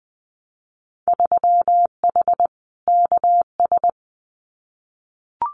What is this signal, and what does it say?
Morse code, heard as a keyed tone, about 20 wpm, 700 Hz. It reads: 3H KS
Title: Winter Field Day Exchanges 20wpm